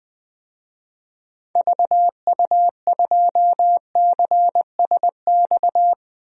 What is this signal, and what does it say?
Morse code, heard as a keyed tone, about 20 wpm, 700 Hz. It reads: VU2CSX
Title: DXCC-100 Call Signs - 6 Characters - 20wpm